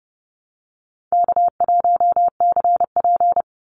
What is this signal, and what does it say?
Morse code, heard as a keyed tone, about 30 wpm, 700 Hz. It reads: K1CP
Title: Call Signs - 30wpm